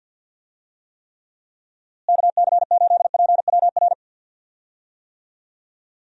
Morse code, heard as a keyed tone, about 52 wpm, 700 Hz. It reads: KC8PWR
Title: USA Call Signs - 6 Characters - Repeated 3x - 50wpm